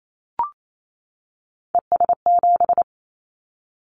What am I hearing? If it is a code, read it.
Morse code, heard as a keyed tone, about 28 wpm, 700 Hz. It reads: ES7
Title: Callsign Prefixes for Top 100 DXCC - 28wpm